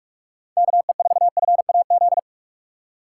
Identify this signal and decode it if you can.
Morse code, heard as a keyed tone, about 46 wpm, 690 Hz. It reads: KE4FAZ
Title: USA Call Signs - 6 Characters - 3x Character Spacing - 45wpm